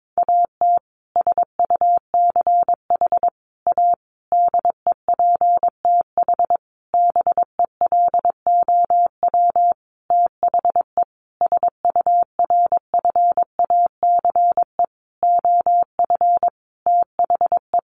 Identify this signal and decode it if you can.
Morse code, heard as a keyed tone, about 22 wpm, 700 Hz. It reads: AT SUCH A DEPTH BELOW THE SURFACE OF THE
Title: Sentences from Top 1200 Words 22wpm